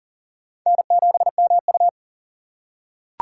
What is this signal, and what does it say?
Morse code, heard as a keyed tone, about 40 wpm, 700 Hz. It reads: N7MU
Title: USA Call Signs - 4 Characters - 4x Character Spacing - 40wpm